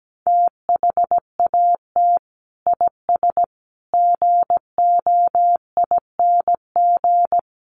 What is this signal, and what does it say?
Morse code, heard as a keyed tone, about 17 wpm, 700 Hz. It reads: THAT IS GOING
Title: Sets of 5 Words Encore2 17wpm